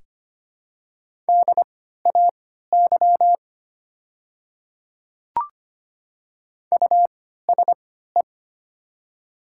Morse code, heard as a keyed tone, about 25 wpm, spaced as for 14 wpm, 700 Hz. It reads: DAY USE
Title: 3x Character Spacing - Top 200 Words - 25wpm